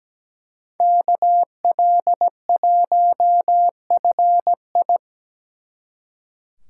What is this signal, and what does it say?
Morse code, heard as a keyed tone, about 20 wpm, 700 Hz. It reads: KL1FI
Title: Speed Racing - USA 2x2 Call Signs 17wpm